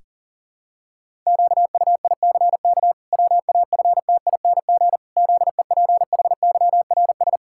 Morse code, heard as a keyed tone, about 37 wpm, 700 Hz. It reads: QUICK WAFTING ZEPHYRS
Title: Pangram Challenge v2 - 40wpm